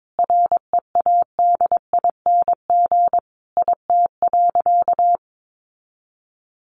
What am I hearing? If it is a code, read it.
Morse code, heard as a keyed tone, about 22 wpm, 700 Hz. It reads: READING IT.
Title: Sentences from Top 1300 Words 22wpm